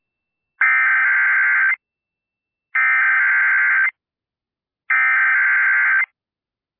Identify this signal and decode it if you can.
Emergency Alert System (SAME) tones (header bursts), heard as harsh buzzing bursts. It is heard from 0.6 to 6.0 s.